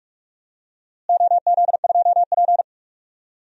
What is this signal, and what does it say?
Morse code, heard as a keyed tone, about 46 wpm, 700 Hz. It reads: OZ2P
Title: DXCC-100 Call Signs - 4 Characters - Repeated 3x - 45wpm